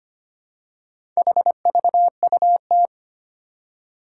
Morse code, heard as a keyed tone, about 25 wpm, 700 Hz. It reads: HVUT
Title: Four Challenging Letter-Number Combinations 25wpm